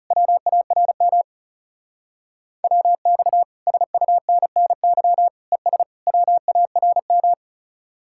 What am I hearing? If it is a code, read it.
Morse code, heard as a keyed tone, about 37 wpm, 700 Hz. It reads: WARM WX SUNNY ES WARM
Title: Speed-Racing Common QSO Phrases - 30wpm